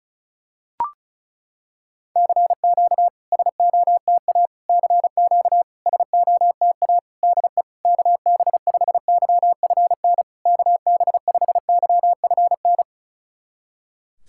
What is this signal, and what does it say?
Morse code, heard as a keyed tone, about 36 wpm, 700 Hz. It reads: CQ SOTA CQ SOTA DE KB5YFN KB5YFN
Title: SOTA QSO Practice 35wpm